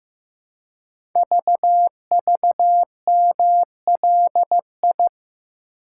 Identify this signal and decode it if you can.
Morse code, heard as a keyed tone, about 15 wpm, 700 Hz. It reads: VVMLI
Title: Code Groups - 5 Letters - 15wpm